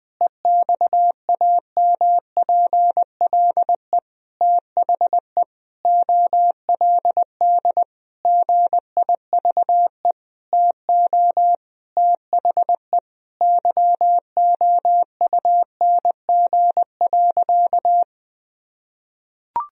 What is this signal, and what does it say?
Morse code, heard as a keyed tone, about 20 wpm, 700 Hz. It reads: EXAMPLE THE OLD GIVE TO THE YOUNG.
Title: Sentences from Top 1000 Words - Review Part 1 - 20wpm